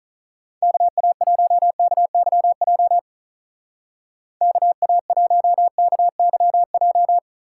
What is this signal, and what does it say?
Morse code, heard as a keyed tone, about 40 wpm, 700 Hz. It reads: KA1KYJ KA1KYJ
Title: Speed Racing - USA Call Signs 35wpm